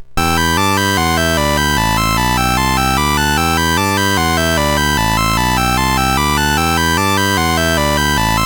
There is no code.